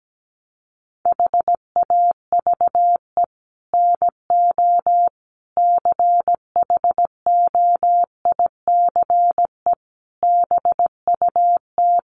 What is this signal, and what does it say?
Morse code, heard as a keyed tone, about 17 wpm, 700 Hz. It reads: HAVE NO CHOICE BUT
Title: Sets of 5 Words Encore 17wpm